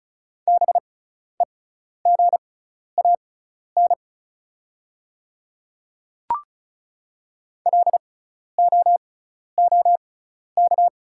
Morse code, heard as a keyed tone, about 36 wpm, spaced as for 12 wpm, 690 Hz. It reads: BEGAN LOOK
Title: 6x Character Spacing - Top 300 Words - 35wpm